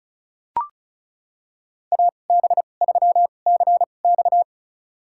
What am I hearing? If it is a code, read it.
Morse code, heard as a keyed tone, about 34 wpm, spaced as for 25 wpm, 700 Hz. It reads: AB3CX
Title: USA Call Signs - 2x Character Spacing - 35wpm